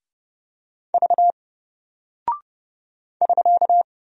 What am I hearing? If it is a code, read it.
Morse code, heard as a keyed tone, about 30 wpm, 700 Hz. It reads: V <SK>
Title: Single Character (Letter:Number:Punctuation:Prosign) Alt 30wpm